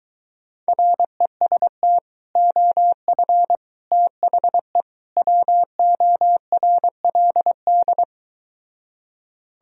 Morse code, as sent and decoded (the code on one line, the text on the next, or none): .-. ... - / --- ..-. / - .... / .-- --- .-. .-.. -..
REST OF THE WORLD